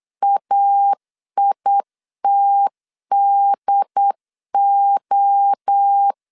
Morse code, heard as a keyed tone, about 8 wpm, 800 Hz. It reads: AITDO
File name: CW / DET / 1 3 1500C S B 8wpm 4141